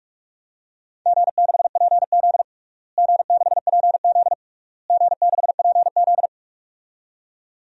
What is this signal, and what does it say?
Morse code, heard as a keyed tone, about 42 wpm, 700 Hz. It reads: G6PZ G6PZ G6PZ
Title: DXCC-100 Call Signs - 4 Characters - Repeated 3x - 45wpm